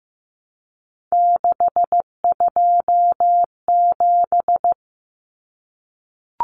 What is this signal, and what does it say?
Morse code, heard as a keyed tone, about 15 wpm, 700 Hz. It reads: 627
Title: Numbers - 3-Digit 15wpm